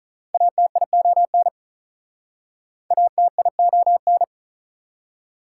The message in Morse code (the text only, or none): ATION ATION